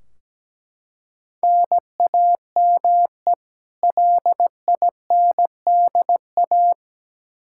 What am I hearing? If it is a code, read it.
Morse code, heard as a keyed tone, about 17 wpm, 700 Hz. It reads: NAME LINDA
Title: Practice QSO - Exchange 1 - Beginning - 17wpm